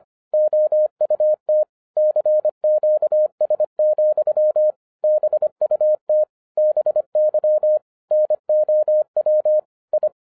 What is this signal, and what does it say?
Morse code, heard as a keyed tone, about 24 wpm, 610 Hz. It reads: OUT CQS, BUT BY NOW I
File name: FIRST FISTS CONTACT-25wpm